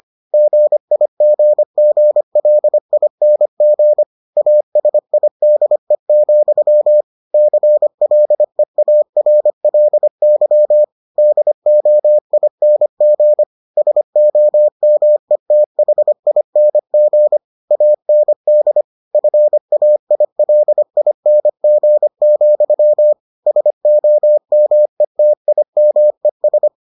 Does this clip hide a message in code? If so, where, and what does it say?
Yes: Morse code from 0.3 to 26.7 s, heard as a keyed tone, about 25 wpm, 600 Hz. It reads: GIGGLING ASIDE, CLEARLY DOING SOMETHING AND FAILING, SOMETIMES